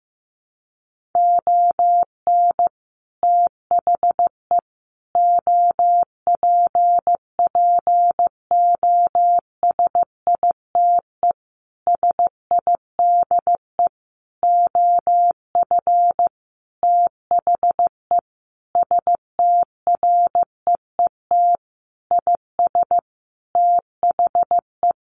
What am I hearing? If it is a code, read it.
Morse code, heard as a keyed tone, about 15 wpm, 700 Hz. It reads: ON THE OPPOSITE SIDE OF THE STREET IS THE
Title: Sentences from Top 1400 Words 15wpm